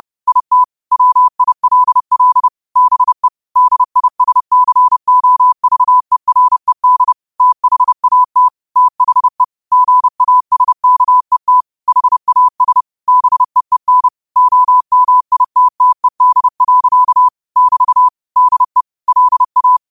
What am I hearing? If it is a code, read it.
Morse code, heard as a keyed tone, about 30 wpm, 1000 Hz. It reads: IT WILL BE DISCOVERED THAT THE GASKET HAS BEEN OMITTED. <BT> DE LA